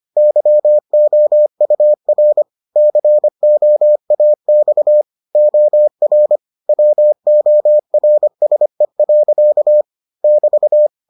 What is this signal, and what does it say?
Morse code, heard as a keyed tone, about 25 wpm, 600 Hz. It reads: YOUR COAX OR WORSE. <BT>